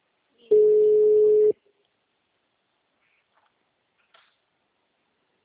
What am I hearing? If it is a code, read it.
Ringback tone (Europe, 425 Hz).